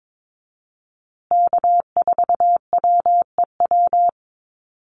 Morse code, heard as a keyed tone, about 22 wpm, 700 Hz. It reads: K4WEW
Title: USA Call Signs Repeated 2x 22wpm